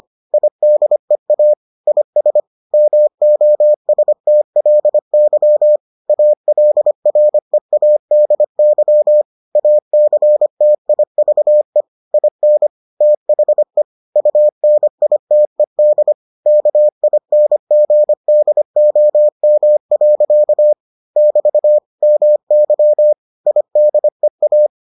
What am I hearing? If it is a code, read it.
Morse code, heard as a keyed tone, about 25 wpm, 600 Hz. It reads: IDEA IS MOSTLY ALREADY ACTIVE IN THE UNITED KINGDOM. <BT> MY IDEA